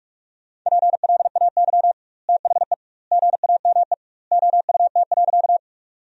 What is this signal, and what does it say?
Morse code, heard as a keyed tone, about 44 wpm, 700 Hz. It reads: PLAY THE GAME OUT.
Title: 2x Word-Spacing - Sentences from Top 500 Words - 45wpm